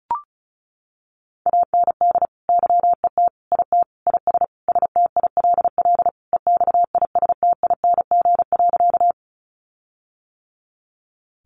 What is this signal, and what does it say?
Morse code, heard as a keyed tone, about 34 wpm, 700 Hz. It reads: AND YET IT IS STILL EXISTING.
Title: Sentences from Top 1500 Words 35wpm